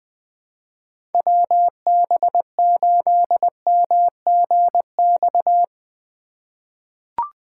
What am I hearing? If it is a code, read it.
Morse code, heard as a keyed tone, about 20 wpm, 700 Hz. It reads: WB8MGX
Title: USA Call Signs - 4x Character Spacing - 20wpm